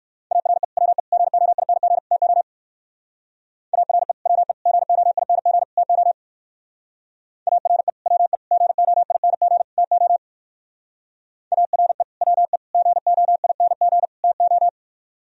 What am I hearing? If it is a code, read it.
Morse code, heard as a keyed tone, about 63 wpm, 700 Hz. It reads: ARE WE GOING TO ARE WE GOING TO ARE WE GOING TO ARE WE GOING TO